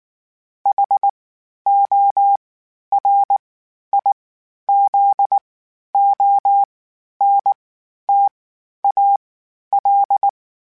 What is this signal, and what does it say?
Morse code, heard as a keyed tone, about 19 wpm, spaced as for 11 wpm, 800 Hz. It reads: HORIZONTAL
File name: (spaced) / FromSMs spaced mp3s / Antennas sp@18